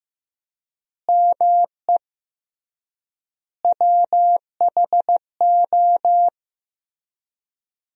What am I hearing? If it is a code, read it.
Morse code, heard as a keyed tone, about 15 wpm, 700 Hz. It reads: ME WHO
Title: 3x Word-Spacing - Sentences from Top 200 Words - 15wpm